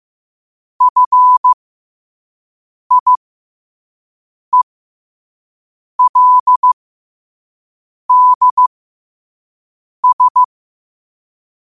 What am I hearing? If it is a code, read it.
Morse code, heard as a keyed tone, about 15 wpm, spaced as for 5 wpm, 1000 Hz. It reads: FIELDS
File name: qotdfiles / 5wpm